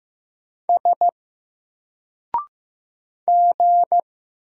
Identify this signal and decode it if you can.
Morse code, heard as a keyed tone, about 15 wpm, 700 Hz. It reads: S G
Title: Single Character (Letter:Number:Punctuation:Prosign) Alt 15wpm